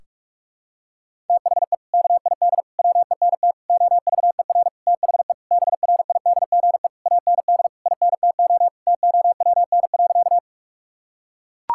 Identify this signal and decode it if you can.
Morse code, heard as a keyed tone, about 42 wpm, 700 Hz. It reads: THE KID WENT OVER THE BRIDGE AND INTO TOWN.
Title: Sentences from Top 1600 Words 45wpm